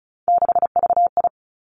Morse code, heard as a keyed tone, about 36 wpm, 700 Hz. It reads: 6VI
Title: Three Challenging Letter-Number Combinations 35wpm